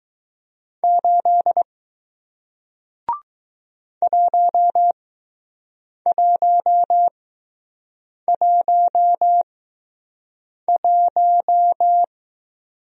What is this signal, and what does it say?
Morse code, heard as a keyed tone, about 20 wpm, spaced as for 6 wpm, 700 Hz. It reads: 8 1111